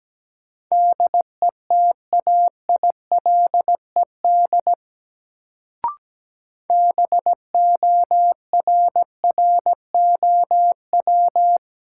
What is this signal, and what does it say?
Morse code, heard as a keyed tone, about 17 wpm, 700 Hz. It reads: DETAILED BORROW
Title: Sentences from Top 1500 Words - New Word Drill Alt - 17wpm